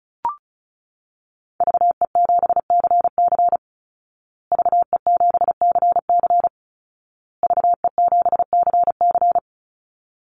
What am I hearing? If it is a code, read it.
Morse code, heard as a keyed tone, about 34 wpm, 700 Hz. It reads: VE7CC VE7CC VE7CC
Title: Call Signs Repeated 2x - 35wpm